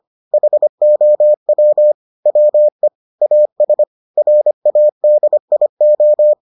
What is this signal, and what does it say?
Morse code, heard as a keyed tone, about 25 wpm, 600 Hz. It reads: HOW WE AS RADIO